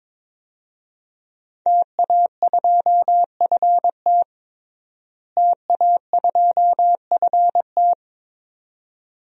Morse code, heard as a keyed tone, about 22 wpm, 700 Hz. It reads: TA2FT TA2FT
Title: DXCC-100 Call Signs - 5 Characters - Repeated 2x - 22wpm